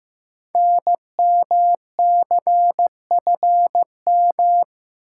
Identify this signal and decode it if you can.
Morse code, heard as a keyed tone, about 15 wpm, 700 Hz. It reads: NMCFM
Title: Code Groups - 5 Letters - 15wpm